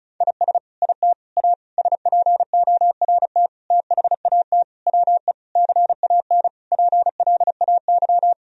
Morse code, heard as a keyed tone, about 34 wpm, 700 Hz. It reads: IS IT A SPORT THAT WE CAN PLAY
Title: Sentences from Top 1000 Words - Review Part 4 - 35wpm